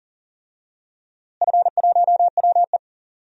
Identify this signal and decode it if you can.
Morse code, heard as a keyed tone, about 40 wpm, 690 Hz. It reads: F1WE
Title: DXCC-100 Call Signs - 4 Characters - Repeated 2x - 40wpm